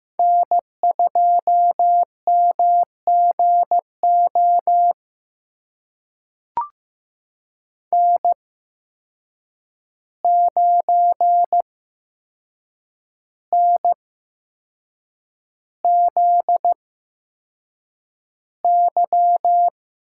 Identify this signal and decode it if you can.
Morse code, heard as a keyed tone, about 15 wpm, spaced as for 6 wpm, 700 Hz. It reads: N2MGO N9NZY
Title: USA Call Signs - 8x Character Spacing - 15wpm